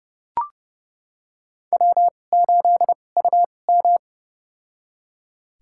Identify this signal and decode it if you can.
Morse code, heard as a keyed tone, about 30 wpm, spaced as for 22 wpm, 700 Hz. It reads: W8UM
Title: USA Call Signs - 2x Character Spacing - 30wpm